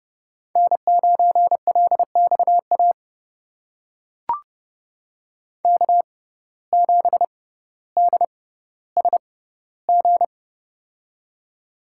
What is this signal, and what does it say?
Morse code, heard as a keyed tone, about 30 wpm, spaced as for 15 wpm, 700 Hz. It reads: N9LXA K7DSG
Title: USA Call Signs - 5 Characters - 6x Character Spacing - 30wpm